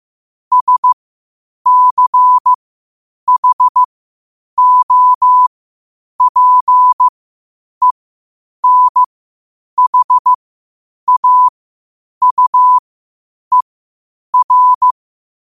Morse code, heard as a keyed tone, about 15 wpm, spaced as for 8 wpm, 1000 Hz. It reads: SCHOPENHAUER
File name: qotdfiles / 8wpm